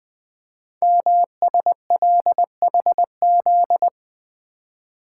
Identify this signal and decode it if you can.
Morse code, heard as a keyed tone, about 20 wpm, 700 Hz. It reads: MSLHZ
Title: Code Groups - 5 Letter:Numbers - 20wpm